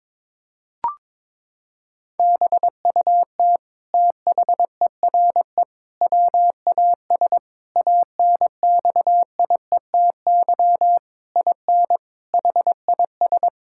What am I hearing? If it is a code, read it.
Morse code, heard as a keyed tone, about 21 wpm, 700 Hz. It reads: BUT THERE WAS ANXIETY IN HIS
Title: Sentences from Top 1600 Words 22wpm